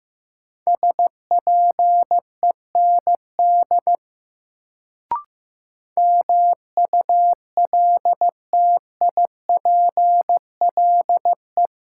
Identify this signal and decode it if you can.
Morse code, heard as a keyed tone, about 15 wpm, 700 Hz. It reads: SPEND MULTIPLE
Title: Sets of 5 Words - New Word Drill - 15wpm